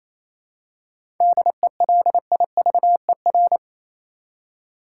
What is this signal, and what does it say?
Morse code, heard as a keyed tone, about 28 wpm, 700 Hz. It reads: DELIVER
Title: Seven Letter Words - 28wpm